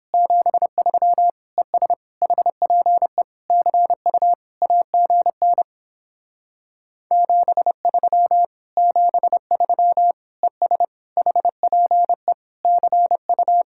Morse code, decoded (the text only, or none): 73 ES HPE CU AGN 73 73 ES HPE CU